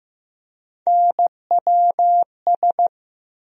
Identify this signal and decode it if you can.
Morse code, heard as a keyed tone, about 15 wpm, 700 Hz. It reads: NWS
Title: USA Call Sign Postfixes - 3 Letters 15wpm